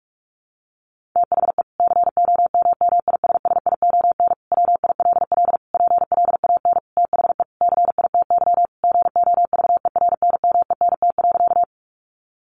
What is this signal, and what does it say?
Morse code, heard as a keyed tone, about 44 wpm, 700 Hz. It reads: THE COMMISSION WILL PLAN THE CITY GOVERNMENT.
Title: Sentences from Top 1400 Words 45wpm